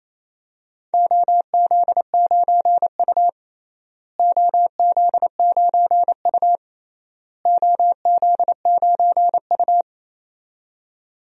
Morse code, heard as a keyed tone, about 28 wpm, 700 Hz. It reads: OZ9U OZ9U OZ9U
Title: DXCC-100 Call Signs - 4 Characters - Repeated 3x - 28wpm